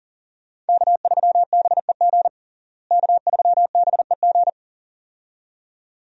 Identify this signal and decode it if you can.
Morse code, heard as a keyed tone, about 40 wpm, 700 Hz. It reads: K3BEG K3BEG
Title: USA Call Signs - 5 Characters - Repeated 2x - 40wpm